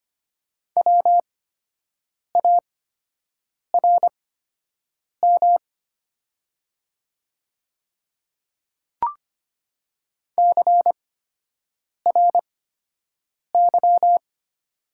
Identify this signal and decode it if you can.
Morse code, heard as a keyed tone, about 25 wpm, spaced as for 7 wpm, 700 Hz. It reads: WARM CRY
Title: 8x Character Spacing - Top 500 Words - 25wpm